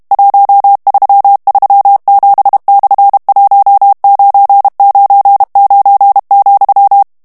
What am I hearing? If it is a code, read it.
Morse code, heard as a keyed tone, about 32 wpm, 800 Hz. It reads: 1337/1999,
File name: ESPAGNOL TEXTE EN CLAIR 2 VITESSE 30 MOTS MINUTES